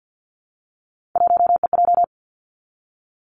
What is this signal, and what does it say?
Morse code, heard as a keyed tone, about 50 wpm, 700 Hz. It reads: 1EJ
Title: Three Challenging Letter-Number Combinations 50wpm